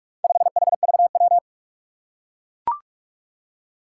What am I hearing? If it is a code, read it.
Morse code, heard as a keyed tone, about 44 wpm, 700 Hz. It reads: 5HVW
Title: Four Challenging Letter-Number Combinations 45wpm